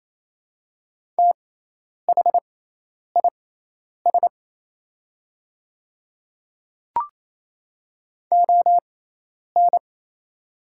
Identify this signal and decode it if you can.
Morse code, heard as a keyed tone, about 28 wpm, spaced as for 10 wpm, 700 Hz. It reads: THIS ON
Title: 6x Character Spacing - Top 100 Words - 28wpm